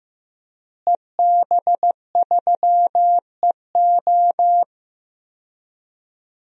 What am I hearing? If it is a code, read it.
Morse code, heard as a keyed tone, about 16 wpm, 700 Hz. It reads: EB3EO
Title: DXCC-100 Call Signs - 5 Characters - Repeated 2x - 15wpm